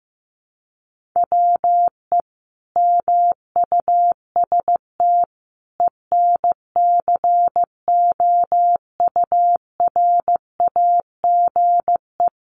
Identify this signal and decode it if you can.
Morse code, heard as a keyed tone, about 15 wpm, 700 Hz. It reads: WE MUST ENCOURAGE
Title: Sentences from Top 1400 Words 15wpm